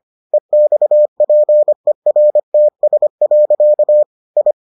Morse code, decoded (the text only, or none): EXPERTS. I